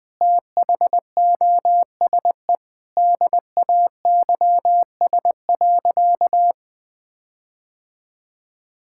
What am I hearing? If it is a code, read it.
Morse code, heard as a keyed tone, about 20 wpm, 700 Hz. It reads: THOSE DAYS.
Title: Sentences with 4-Word Phrases - 20wpm